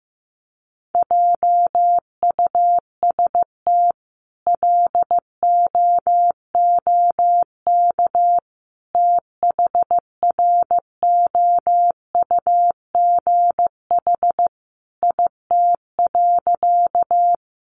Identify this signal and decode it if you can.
Morse code, heard as a keyed tone, about 15 wpm, 700 Hz. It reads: JUST LOOK THROUGH IT.
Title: Sentences from Top 200 Words 15wpm